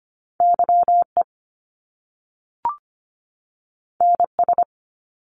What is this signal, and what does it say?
Morse code, heard as a keyed tone, about 25 wpm, 700 Hz. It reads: YE NS
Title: Top Two Letter Combinations in Words 25wpm